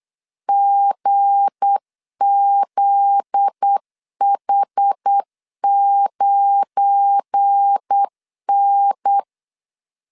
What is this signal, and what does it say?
Morse code, heard as a keyed tone, about 8 wpm, 800 Hz. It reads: GZH9N
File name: LEZIONI / 9 2 8wpm 3437